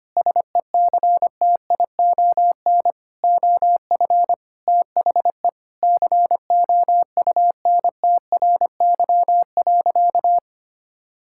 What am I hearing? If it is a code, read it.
Morse code, heard as a keyed tone, about 25 wpm, 700 Hz. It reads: SECTION OF THE COUNTRY.
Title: Sentences from Top 1000 Words - Review Part 5 - 25wpm